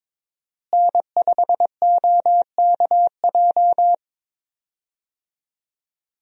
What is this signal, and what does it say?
Morse code, heard as a keyed tone, about 22 wpm, 700 Hz. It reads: N5OKJ
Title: USA Call Signs - 5 Characters - Repeated 2x - 22wpm